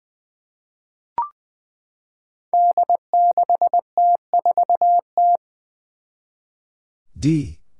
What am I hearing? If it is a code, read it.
Morse code, heard as a keyed tone, about 20 wpm, 700 Hz. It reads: D6T4T
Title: Five Challenging Letter-Number Combinations 20wpm